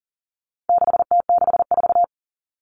Morse code, heard as a keyed tone, about 40 wpm, 700 Hz. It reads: 6T64
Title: All Cut Numbers - 4-Digit 40wpm